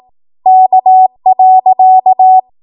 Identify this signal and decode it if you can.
Morse code, heard as a keyed tone, about 18 wpm, 750 Hz. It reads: K.
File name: Morse / 18WPM